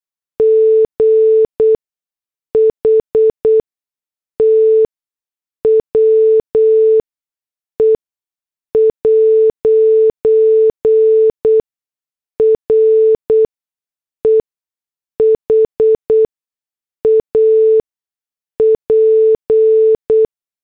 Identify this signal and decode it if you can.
Morse code, heard as a keyed tone, about 8 wpm, spaced as for 6 wpm, 440 Hz. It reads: GHT WE'RE HAP